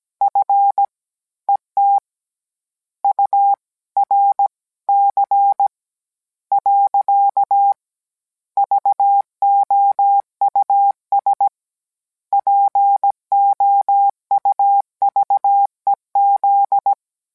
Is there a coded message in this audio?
Yes: Morse code from 0.2 to 16.9 s, heard as a keyed tone, about 17 wpm, 800 Hz. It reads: F ET U R C . VOUS POUVEZ